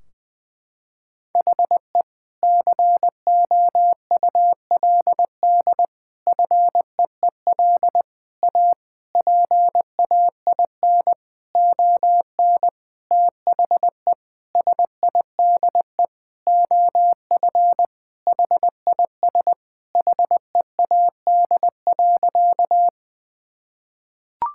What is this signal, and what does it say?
Morse code, heard as a keyed tone, about 20 wpm, 700 Hz. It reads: HE COULD FEEL A PAIN ON THE SIDE OF HIS HEAD.
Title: Sentences from Top 1600 Words 20wpm